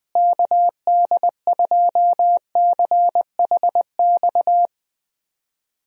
Morse code, heard as a keyed tone, about 20 wpm, 700 Hz. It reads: KD2CHX
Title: USA Call Signs - 6 Characters - Repeated 2x - 20wpm